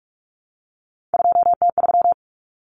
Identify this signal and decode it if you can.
Morse code, heard as a keyed tone, about 44 wpm, 700 Hz. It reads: 2T3
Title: Common Cut Numbers - 3-Digit 45wpm